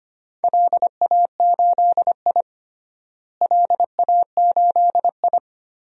Morse code, heard as a keyed tone, about 25 wpm, 700 Hz. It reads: LA8I LA8I
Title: DXCC-100 Call Signs - 4 Characters - Repeated 2x - 25wpm